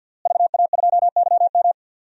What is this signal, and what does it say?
Morse code, heard as a keyed tone, about 53 wpm, 700 Hz. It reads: VA2YM
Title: DXCC-100 Call Signs - 5 Characters - Repeated 3x - 50wpm